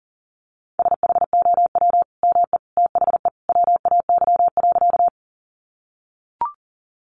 Morse code, heard as a keyed tone, about 40 wpm, 700 Hz. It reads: SHOW ME THE WAY.
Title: Sentences from Top 200 Words 40wpm